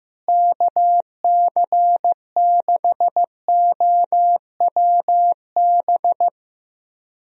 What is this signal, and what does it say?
Morse code, heard as a keyed tone, about 15 wpm, 700 Hz. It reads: KC6OWB
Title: USA Call Signs - 6 Characters - Repeated 3x - 15wpm